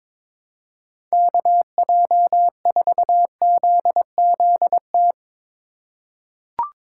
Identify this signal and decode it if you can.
Morse code, heard as a keyed tone, about 22 wpm, 700 Hz. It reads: KJ4ZZT
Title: USA Call Signs - 6 Characters - 4x Character Spacing - 22wpm